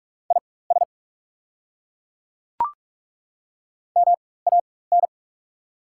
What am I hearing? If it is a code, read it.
Morse code, heard as a keyed tone, about 43 wpm, spaced as for 21 wpm, 690 Hz. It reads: IS MAN
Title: 4x Character Spacing - Top 200 Words - 45wpm